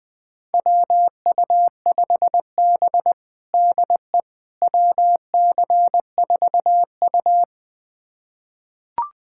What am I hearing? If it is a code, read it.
Morse code, heard as a keyed tone, about 21 wpm, 700 Hz. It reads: WU5B DE WC4U
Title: USA Callsign DE Callsign - 4 Characters - 20wpm